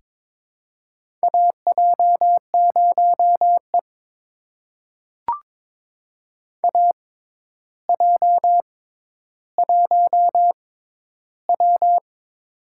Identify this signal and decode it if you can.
Morse code, heard as a keyed tone, about 22 wpm, spaced as for 11 wpm, 700 Hz. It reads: AJ0E AJ1W
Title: USA Call Signs - 4 Characters - 6x Character Spacing - 22wpm